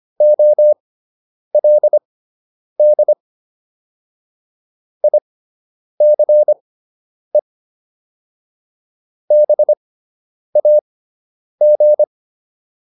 Morse code, heard as a keyed tone, about 25 wpm, spaced as for 9 wpm, 600 Hz. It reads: OLD ICE BAG